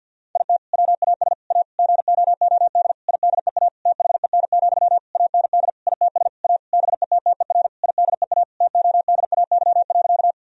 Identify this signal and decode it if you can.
Morse code, heard as a keyed tone, about 50 wpm, 700 Hz. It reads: IT WAS A GOOD IDEA THEN, AND ITS A BETTER IDEA TODAY.
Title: Sentences from Top 1000 Words - Review Part 4 - 50wpm